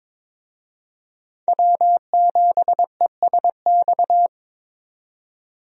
Morse code, heard as a keyed tone, about 21 wpm, 700 Hz. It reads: W7ESX
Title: USA Call Signs - 5 Characters - 22wpm